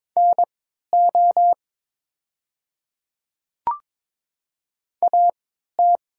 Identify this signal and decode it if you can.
Morse code, heard as a keyed tone, about 22 wpm, spaced as for 12 wpm, 700 Hz. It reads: NO AT
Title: 3x Character Spacing - Top 100 Words - 22wpm